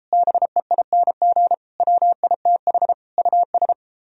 Morse code, heard as a keyed tone, about 37 wpm, 700 Hz. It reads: BEING WITH US